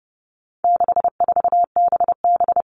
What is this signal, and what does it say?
Morse code, heard as a keyed tone, about 30 wpm, 700 Hz. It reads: 64BB
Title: All Cut Numbers - 4-Digit 30wpm